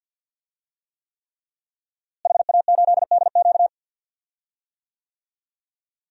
Morse code, heard as a keyed tone, about 51 wpm, 700 Hz. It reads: HA8DQ
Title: DXCC-100 Call Signs - 5 Characters - Repeated 3x - 50wpm